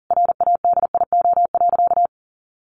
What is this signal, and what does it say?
Morse code, heard as a keyed tone, about 40 wpm, 700 Hz. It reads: RADIO.
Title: A Sound Of Thunder 40wpm